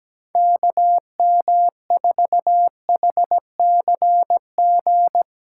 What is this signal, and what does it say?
Morse code, heard as a keyed tone, about 17 wpm, 700 Hz. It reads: KM4HCG
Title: USA Callsign DE Callsign 17wpm